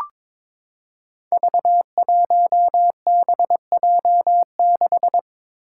Morse code, heard as a keyed tone, about 22 wpm, 700 Hz. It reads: V1BJ6
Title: Five Challenging Letter-Number Combinations 22wpm